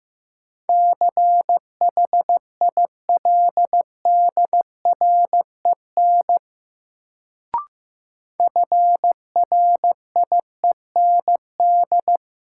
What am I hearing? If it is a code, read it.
Morse code, heard as a keyed tone, about 15 wpm, 700 Hz. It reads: CHILDREN FRIEND
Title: Sentences from Top 300 Words - New Word Drill - 15wpm